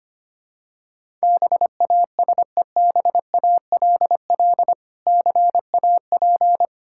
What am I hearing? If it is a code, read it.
Morse code, heard as a keyed tone, about 25 wpm, 700 Hz. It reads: BASEBALL CAP
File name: practice / 25wpm / Difficult Words with Hints - 25wpm